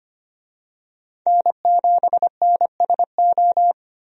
Morse code, heard as a keyed tone, about 25 wpm, 700 Hz. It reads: N7NSO
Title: USA Call Signs - 5 Characters - 25wpm